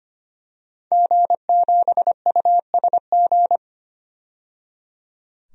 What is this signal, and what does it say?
Morse code, heard as a keyed tone, about 25 wpm, 700 Hz. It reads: G7USG
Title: DXCC-100 Call Signs - 5 Characters - 25wpm